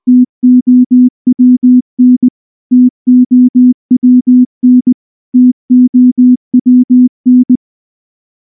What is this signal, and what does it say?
Morse code, heard as a keyed tone, about 19 wpm, 250 Hz. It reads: TOWN TOWN TOWN
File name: morse / morse-training-20WPM